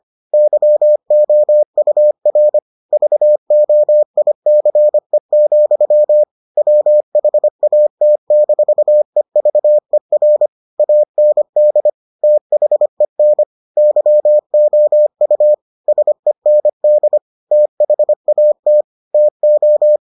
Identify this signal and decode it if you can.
Morse code, heard as a keyed tone, about 25 wpm, 600 Hz. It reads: YOUR VOICE, WHAT-EVER AND THEN YOU SEND THAT TO